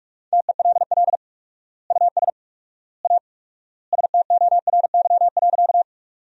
Morse code, heard as a keyed tone, about 46 wpm, 700 Hz. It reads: TELL US A STORY.
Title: 4x Word-Spacing - Sentences from Top 300 Words - 45wpm